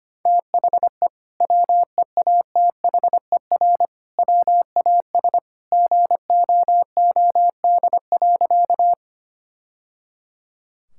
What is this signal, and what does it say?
Morse code, heard as a keyed tone, about 25 wpm, 700 Hz. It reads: THE WEATHER WAS GOOD.
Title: Sentences from Top 1000 Words - Review Part 5 - 25wpm